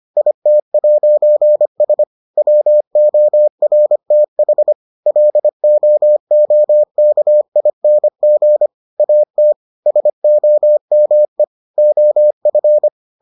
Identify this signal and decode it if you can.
Morse code, heard as a keyed tone, about 25 wpm, 600 Hz. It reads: IT'S WORTH LOOKING AT SOME OF